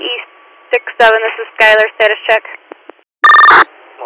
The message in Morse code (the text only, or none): none